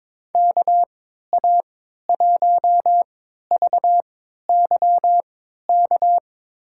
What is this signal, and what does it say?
Morse code, heard as a keyed tone, about 22 wpm, spaced as for 13 wpm, 700 Hz. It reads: KA1VYK
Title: USA Call Signs - 3x Character Spacing - 22wpm